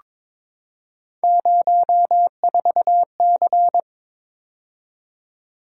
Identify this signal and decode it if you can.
Morse code, heard as a keyed tone, about 21 wpm, 700 Hz. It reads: 04C
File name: practice / 22wpm / Three Letter-Number Combinations - 22wpm